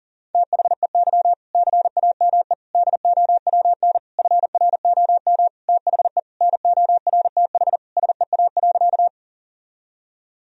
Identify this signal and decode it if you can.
Morse code, heard as a keyed tone, about 37 wpm, 700 Hz. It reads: THEY CAME DOWN FROM THE NORTH SEA.
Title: Sentences from Top 1000 Words - Review Part 1 - 40wpm